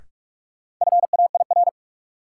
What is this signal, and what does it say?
Morse code, heard as a keyed tone, about 41 wpm, 700 Hz. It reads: FAIR
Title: Sentences from Top 1300 Words - New Word Drill Alt - 45wpm